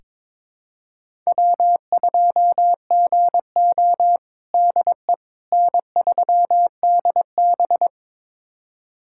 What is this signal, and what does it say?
Morse code, heard as a keyed tone, about 22 wpm, 700 Hz. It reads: W2GO DE N3DB
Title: USA Callsign DE Callsign - 4 Characters - 22wpm